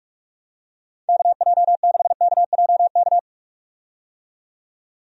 Morse code, heard as a keyed tone, about 46 wpm, 700 Hz. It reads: KJ6KJK
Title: USA Call Signs - 6 Characters - 45wpm